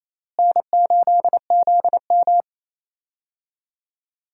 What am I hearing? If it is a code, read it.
Morse code, heard as a keyed tone, about 28 wpm, 700 Hz. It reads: N8ZM
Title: USA Call Signs - 4 Characters - Repeated 3x - 28wpm